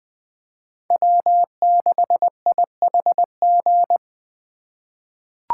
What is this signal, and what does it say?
Morse code, heard as a keyed tone, about 20 wpm, 700 Hz. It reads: W6IHG
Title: K1USN SST QSO Practice - 20wpm